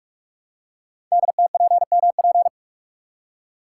Morse code, heard as a keyed tone, about 44 wpm, 690 Hz. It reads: DTPMP
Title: Code Groups - 5 Letters - 45wpm